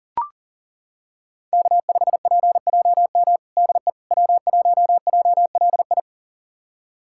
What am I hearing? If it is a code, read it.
Morse code, heard as a keyed tone, about 40 wpm, 700 Hz. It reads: K5PJM DE W1JLI
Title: USA Callsign DE Callsign - 5 Characters - 40wpm